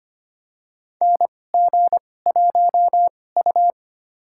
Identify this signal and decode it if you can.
Morse code, heard as a keyed tone, about 25 wpm, spaced as for 18 wpm, 700 Hz. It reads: NG1U
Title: USA Call Signs - 4 Characters - 2x Character Spacing - 25wpm